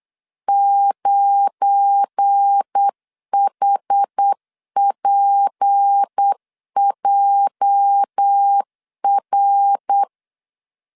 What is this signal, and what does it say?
Morse code, heard as a keyed tone, about 8 wpm, 800 Hz. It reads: 9HPJR
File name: LEZIONI / 10 SIGNS / 10C DOT 8wpm 3649